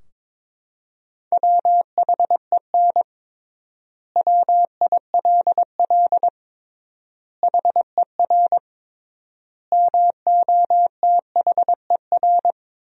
Morse code, heard as a keyed tone, about 22 wpm, 700 Hz. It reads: WHEN WILL HER MOTHER
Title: 3x Word-Spacing - Sentences from Top 200 Words - 22wpm